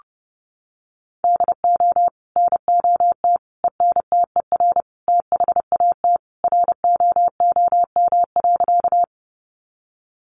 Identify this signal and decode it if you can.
Morse code, heard as a keyed tone, about 30 wpm, 700 Hz. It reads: DO NOT ENTER THAT ROOM.
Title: Sentences from Top 1300 Words 30wpm